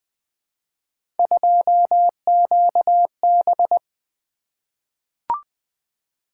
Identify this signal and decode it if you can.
Morse code, heard as a keyed tone, about 20 wpm, 700 Hz. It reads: 2QB
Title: Three Letter-Number Combinations - 20wpm